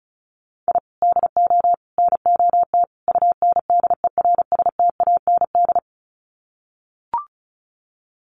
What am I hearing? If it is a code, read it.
Morse code, heard as a keyed tone, about 34 wpm, 700 Hz. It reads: I DO NOT UNDERSTAND
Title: Sets of 4 Words 35wpm